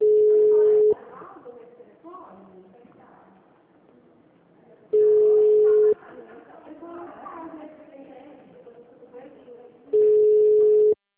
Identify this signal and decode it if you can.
Ringback tone (Europe, 425 Hz), with speech.